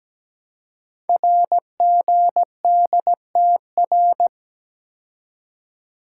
Morse code, heard as a keyed tone, about 17 wpm, 700 Hz. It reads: RGDTR
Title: Code Groups - 5 Letters - 17wpm